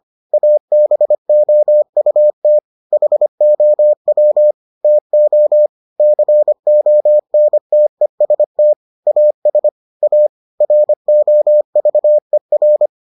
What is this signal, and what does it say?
Morse code, heard as a keyed tone, about 25 wpm, 600 Hz. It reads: ABOUT HOW TO CONTEST AS A ROVER